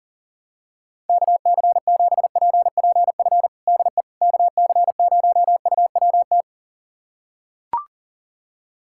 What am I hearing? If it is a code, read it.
Morse code, heard as a keyed tone, about 37 wpm, 700 Hz. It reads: KC7PPF DE KC0UWT
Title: USA Callsign DE Callsign - 6 Characters - 40wpm